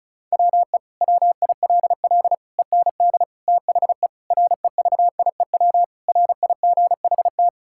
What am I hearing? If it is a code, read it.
Morse code, heard as a keyed tone, about 34 wpm, 690 Hz. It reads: WE WILL END THE REVIEW RIGHT
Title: Sentences from Top 1000 Words - Review Part 6 - 35wpm